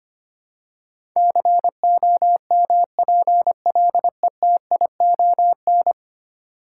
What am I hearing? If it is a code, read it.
Morse code, heard as a keyed tone, about 25 wpm, 700 Hz. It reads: COMPLETION
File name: Ten Letter Words - 25wpm